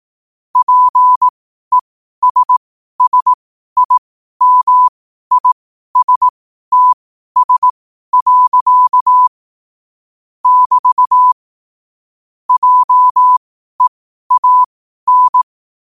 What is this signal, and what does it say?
Morse code, heard as a keyed tone, about 18 wpm, spaced as for 12 wpm, 1000 Hz. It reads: PESSIMISTS. <BT> JEAN
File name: qotdfiles / 12wpm